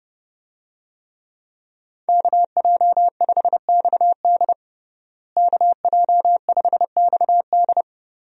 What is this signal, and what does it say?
Morse code, heard as a keyed tone, about 32 wpm, 700 Hz. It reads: KJ5XD KJ5XD
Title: USA Call Signs - 5 Characters - Repeated 2x - 30wpm